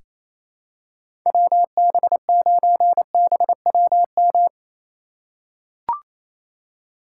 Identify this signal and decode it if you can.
Morse code, heard as a keyed tone, about 27 wpm, 700 Hz. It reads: WB9BWM
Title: USA Call Signs - 4x Character Spacing - 28wpm